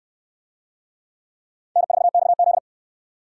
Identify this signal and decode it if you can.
Morse code, heard as a keyed tone, about 60 wpm, 700 Hz. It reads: N4XZ